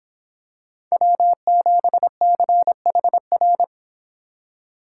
Morse code, heard as a keyed tone, about 26 wpm, 700 Hz. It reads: W7CHR